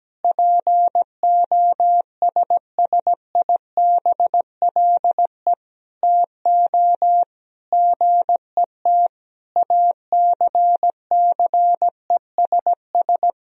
Morse code, heard as a keyed tone, about 17 wpm, 700 Hz. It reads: POSSIBLE TO GET ACCESS
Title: Sentences from Top 1000 Words - Review Part 4 - 17wpm